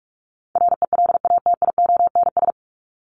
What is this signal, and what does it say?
Morse code, heard as a keyed tone, about 44 wpm, 700 Hz. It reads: RELATIONS
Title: ES:S Suffix - Top 100 Words with Base Words 45wpm